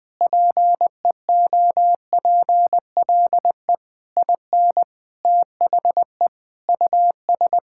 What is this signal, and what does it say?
Morse code, heard as a keyed tone, about 20 wpm, 700 Hz. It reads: PEOPLE IN THE US
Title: Sentences with 7-Word Phrases - 20wpm